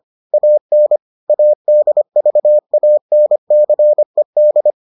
Morse code, heard as a keyed tone, about 25 wpm, 600 Hz. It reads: AN ADVANCED